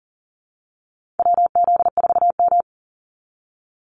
Morse code, heard as a keyed tone, about 40 wpm, 700 Hz. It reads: WZ4M
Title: NA SOTA Activator Callsigns 40wpm